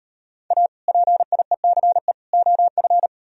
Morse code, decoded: A PIECE OF

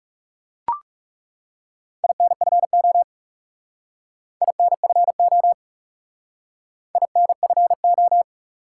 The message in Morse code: INFO INFO INFO

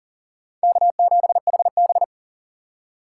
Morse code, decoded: K7HB

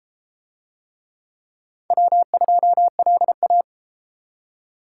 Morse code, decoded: W2LA

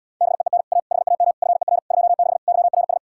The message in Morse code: BEEN A HAM FER 26 YRS

